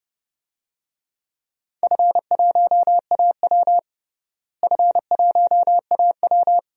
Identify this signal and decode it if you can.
Morse code, heard as a keyed tone, about 30 wpm, 700 Hz. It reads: F1AW F1AW
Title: DXCC-100 Call Signs - 4 Characters - Repeated 2x - 30wpm